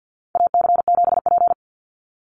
Morse code, heard as a keyed tone, about 48 wpm, 700 Hz. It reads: AC7P
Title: NA SOTA Chaser Callsigns 50wpm